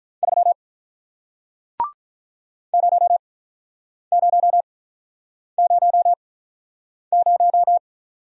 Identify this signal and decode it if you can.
Morse code, heard as a keyed tone, about 48 wpm, spaced as for 9 wpm, 700 Hz. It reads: 3 0000